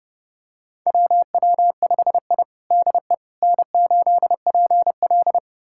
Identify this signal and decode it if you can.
Morse code, heard as a keyed tone, about 32 wpm, 700 Hz. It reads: WW5I DE N8PL
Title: USA Callsign DE Callsign - 4 Characters - 30wpm